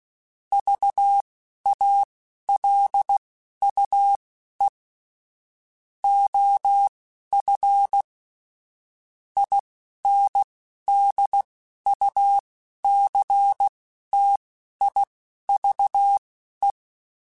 Morse code, heard as a keyed tone, about 16 wpm, spaced as for 11 wpm, 800 Hz. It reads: VALUE OF INDUCTIVE
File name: Technical Subjects / Technical Subjects-mp3s / XL @10